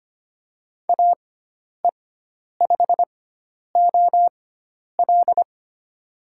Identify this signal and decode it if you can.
Morse code, heard as a keyed tone, about 25 wpm, spaced as for 10 wpm, 700 Hz. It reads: AE5OL
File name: Lesson 16g - Letter c - Call Signs 25wpm